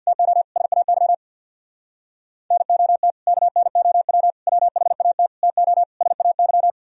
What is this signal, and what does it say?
Morse code, heard as a keyed tone, about 64 wpm, 690 Hz. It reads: TO SAY NOT KNOW WHAT TO SAY